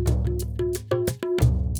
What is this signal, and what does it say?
There is no code.